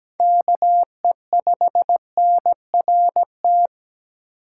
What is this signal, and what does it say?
Morse code, heard as a keyed tone, about 17 wpm, 700 Hz. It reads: KE5NRT
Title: USA Call Signs - 6 Characters - 4x Character Spacing - 17wpm